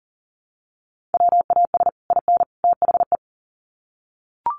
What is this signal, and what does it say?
Morse code, heard as a keyed tone, about 40 wpm, 700 Hz. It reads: WAS IN THE
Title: Sets of 3 Words 40wpm